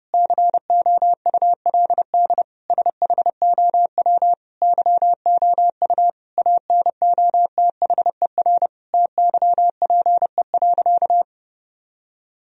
Morse code, heard as a keyed tone, about 32 wpm, 700 Hz. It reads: COULD SHOW YOU ANOTHER TYPE.
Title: Sentences from Top 1000 Words - Review Part 3 - 30wpm